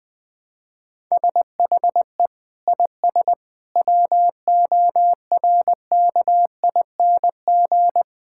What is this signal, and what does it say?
Morse code, heard as a keyed tone, about 20 wpm, 700 Hz. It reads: SHE IS WORKING
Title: Sentences from Top 1000 Words - Review Part 5 - 20wpm